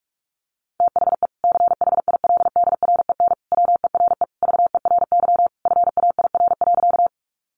Morse code, heard as a keyed tone, about 44 wpm, 700 Hz. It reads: THE CHILDREN WERE VERY FAIR.
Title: Sentences from Top 1300 Words 45wpm